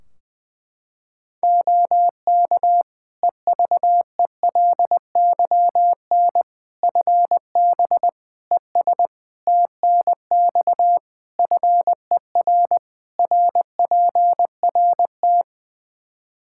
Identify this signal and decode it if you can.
Morse code, heard as a keyed tone, about 20 wpm, 700 Hz. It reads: OK EVELYN FB ES TNX FER RPRT
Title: Practice QSO - Exchange 2 - Info - 20wpm